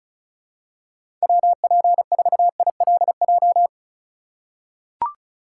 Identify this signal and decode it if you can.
Morse code, heard as a keyed tone, about 36 wpm, 700 Hz. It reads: WP4ILJ
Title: USA Call Signs - 6 Characters - 4x Character Spacing - 35wpm